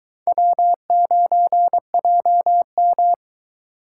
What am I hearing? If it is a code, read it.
Morse code, heard as a keyed tone, about 23 wpm, 700 Hz. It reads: W9JM